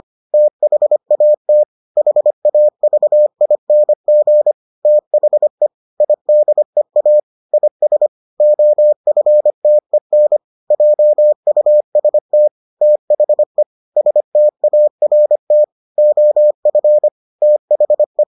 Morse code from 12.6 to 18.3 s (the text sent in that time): THE START OF THE